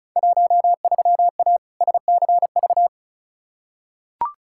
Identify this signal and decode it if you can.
Morse code, heard as a keyed tone, about 36 wpm, 690 Hz. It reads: 13A SCV
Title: Field Day Exchanges Encore - 35wpm